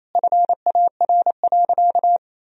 Morse code, heard as a keyed tone, about 28 wpm, 700 Hz. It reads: FAR.
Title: Sentences from Top 400 Words and Abbreviations 28wpm